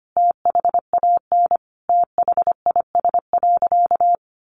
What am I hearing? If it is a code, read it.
Morse code, heard as a keyed tone, about 25 wpm, 700 Hz. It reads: THAN THIS.
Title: Sentences from Top 100 Words 25wpm